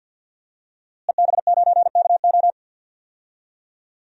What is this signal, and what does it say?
Morse code, heard as a keyed tone, about 52 wpm, 700 Hz. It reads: EB9KO
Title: DXCC-100 Call Signs - 5 Characters - Repeated 3x - 50wpm